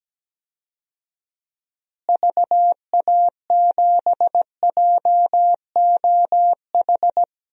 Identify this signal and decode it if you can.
Morse code, heard as a keyed tone, about 17 wpm, 700 Hz. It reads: VA7JOH
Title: DXCC-100 Call Signs - 6 Characters - Repeated 2x - 17wpm